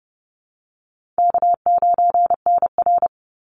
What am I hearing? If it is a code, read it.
Morse code, heard as a keyed tone, about 30 wpm, 700 Hz. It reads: K9NR
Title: Easy Call Signs - 30wpm